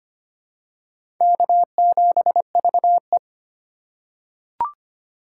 Morse code, heard as a keyed tone, about 25 wpm, 700 Hz. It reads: K7VE
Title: USA Call Signs - 4 Characters - 4x Character Spacing - 25wpm